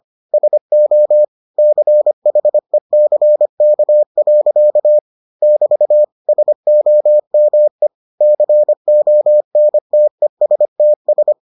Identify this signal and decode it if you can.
Morse code, heard as a keyed tone, about 25 wpm, 600 Hz. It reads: SO CHECK. <BT> SOME CONTESTS